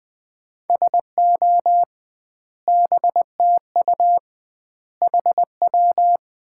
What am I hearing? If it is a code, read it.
Morse code, heard as a keyed tone, about 20 wpm, 700 Hz. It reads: SO BTU HW
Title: Common QSO Phrases - 2x Word-Spacing - 20wpm